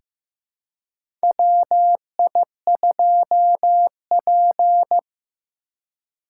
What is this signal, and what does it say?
Morse code, heard as a keyed tone, about 15 wpm, 700 Hz. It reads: WI2P
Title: USA Call Signs - 4 Characters - 3x Character Spacing - 15wpm